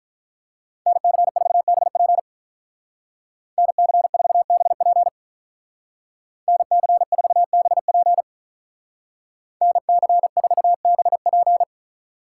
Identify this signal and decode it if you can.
Morse code, heard as a keyed tone, about 47 wpm, 700 Hz. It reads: NC4BP NC4BP NC4BP NC4BP